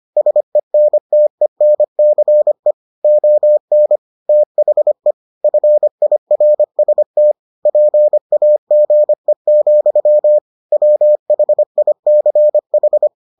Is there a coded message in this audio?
Yes: Morse code from 0.2 to 13.1 s, heard as a keyed tone, about 25 wpm, 600 Hz. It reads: SENTENCE ON THE FIRST PAGE, WHICH